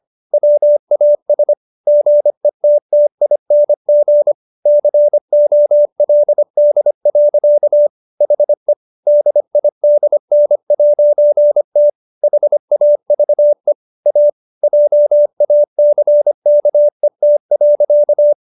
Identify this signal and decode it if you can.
Morse code, heard as a keyed tone, about 25 wpm, 590 Hz. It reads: WAS GETTING COLD. HE DIDN'T HAVE A JACKET.